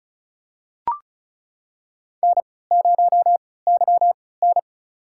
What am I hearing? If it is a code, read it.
Morse code, heard as a keyed tone, about 36 wpm, spaced as for 20 wpm, 700 Hz. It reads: N0YN
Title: USA Call Signs - 4 Characters - 3x Character Spacing - 35wpm